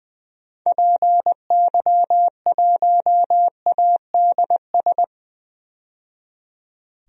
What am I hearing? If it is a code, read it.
Morse code, heard as a keyed tone, about 20 wpm, 700 Hz. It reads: PY1ADS
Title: DXCC-100 Call Signs - 6 Characters - 20wpm